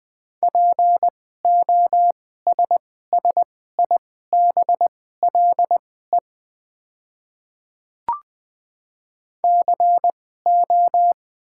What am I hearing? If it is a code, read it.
Morse code, heard as a keyed tone, about 20 wpm, spaced as for 14 wpm, 690 Hz. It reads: POSSIBLE CO
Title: 2x Character Spacing - Top 500 Words - 20wpm